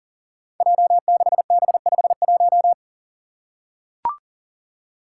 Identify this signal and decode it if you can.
Morse code, heard as a keyed tone, about 39 wpm, 700 Hz. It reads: J6B51
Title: Five Challenging Letter-Number Combinations 40wpm